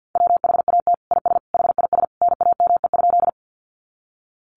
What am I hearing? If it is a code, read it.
Morse code, heard as a keyed tone, about 50 wpm, 700 Hz. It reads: WHAT IS HIS NAME?
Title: Sentences from Top 200 Words 50wpm